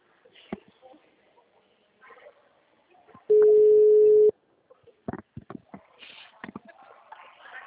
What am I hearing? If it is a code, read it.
Ringback tone (Europe, 425 Hz), with speech.